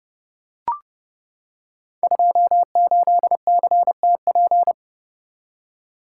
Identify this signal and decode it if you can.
Morse code, heard as a keyed tone, about 31 wpm, 700 Hz. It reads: 28CTP
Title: Code Groups - 5 Letter:Numbers - 30wpm